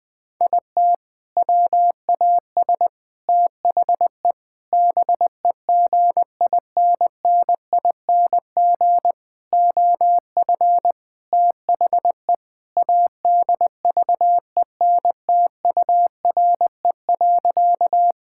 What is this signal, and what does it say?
Morse code, heard as a keyed tone, about 21 wpm, 700 Hz. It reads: IT WAS THE BEGINNING OF THE ADVENTURE.
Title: Sentences with 4-Word Phrases - 20wpm